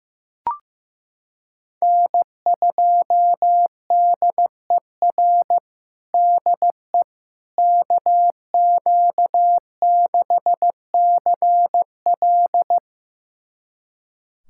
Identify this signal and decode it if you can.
Morse code, heard as a keyed tone, about 15 wpm, 700 Hz. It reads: N2DER DE KQ6CL
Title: USA Callsign DE Callsign - 5 Characters - 15wpm